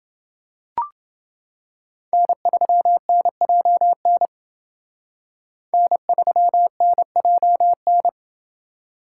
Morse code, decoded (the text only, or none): N3NJN N3NJN